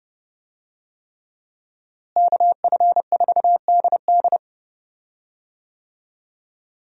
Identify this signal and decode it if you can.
Morse code, heard as a keyed tone, about 30 wpm, 700 Hz. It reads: KF4DD
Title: USA Call Signs - 5 Characters - Repeated 3x - 30wpm